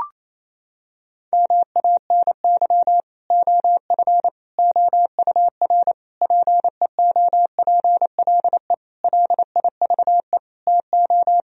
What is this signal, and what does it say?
Morse code, heard as a keyed tone, about 28 wpm, 700 Hz. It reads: MANY OF OUR PEOPLE LIVE TO
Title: Sentences from Top 1000 Words - Review Part 3 - 28wpm